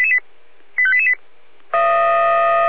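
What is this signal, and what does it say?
Five-tone selective call (ZVEI), heard as quick runs of five tones, tones of about 70 ms. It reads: 85888
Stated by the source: Five-tone ZVEI selective call received on an SDR.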